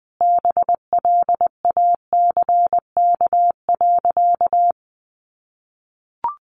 Morse code, heard as a keyed tone, about 20 wpm, 700 Hz. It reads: BLACK.
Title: Sentences from Top 1200 Words 20wpm